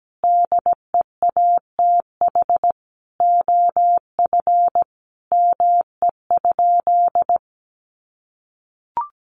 Morse code, heard as a keyed tone, about 17 wpm, 700 Hz. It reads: DEATH OF ME?
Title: Sentences from Top 1100 Words 17wpm